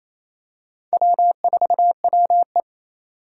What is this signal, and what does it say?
Morse code, heard as a keyed tone, about 28 wpm, 700 Hz. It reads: W4WE
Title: Four Challenging Letter-Number Combinations 28wpm